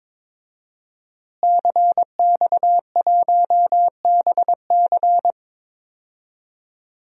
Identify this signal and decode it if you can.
Morse code, heard as a keyed tone, about 22 wpm, 700 Hz. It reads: CX1BC
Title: DXCC-100 Call Signs - 5 Characters - Repeated 2x - 22wpm